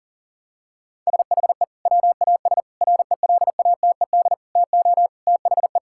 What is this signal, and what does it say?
Morse code, heard as a keyed tone, about 37 wpm, 700 Hz. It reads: SHE WAS RELATED TO THE
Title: Sentences from Top 1000 Words - Review Part 5 - 40wpm